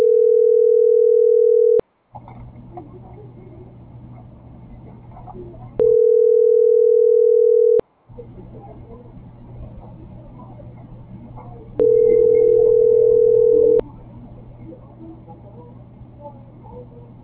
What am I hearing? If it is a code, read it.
Ringback tone (North America), 440 + 480 Hz.